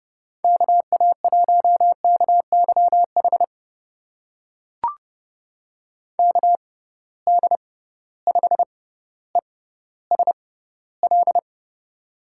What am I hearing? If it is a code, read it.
Morse code, heard as a keyed tone, about 32 wpm, spaced as for 16 wpm, 700 Hz. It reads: KA1KYH KD5ESL
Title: USA Call Signs - 6 Characters - 6x Character Spacing - 30wpm